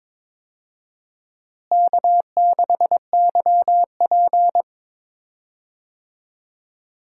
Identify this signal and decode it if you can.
Morse code, heard as a keyed tone, about 22 wpm, 700 Hz. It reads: K6YP
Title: USA Call Signs - 4 Characters - Repeated 2x - 22wpm